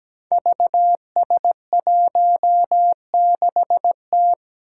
Morse code, heard as a keyed tone, about 17 wpm, 700 Hz. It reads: VS16T
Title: Five Challenging Letter-Number Combinations 17wpm